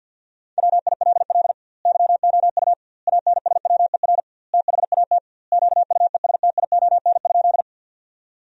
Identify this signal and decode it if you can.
Morse code, heard as a keyed tone, about 50 wpm, 700 Hz. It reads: WILL YOU ANSWER THAT QUESTION?
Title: 2x Word-Spacing - Sentences from Top 400 Words - 50wpm